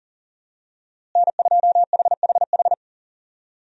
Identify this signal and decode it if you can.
Morse code, heard as a keyed tone, about 40 wpm, 700 Hz. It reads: N2HHH
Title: USA Call Signs - 5 Characters - Repeated 2x - 40wpm